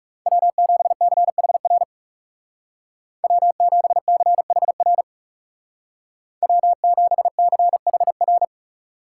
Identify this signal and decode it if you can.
Morse code, heard as a keyed tone, about 36 wpm, 700 Hz. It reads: W7CHR W7CHR W7CHR
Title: Speed Racing - USA 1x3 Call Signs 30wpm